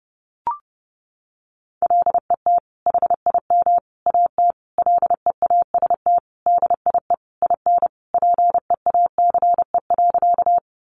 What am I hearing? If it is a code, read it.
Morse code, heard as a keyed tone, about 28 wpm, 700 Hz. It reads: LET HIM AT LEAST DIE IN PEACE.
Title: Sentences from Top 1500 Words 30wpm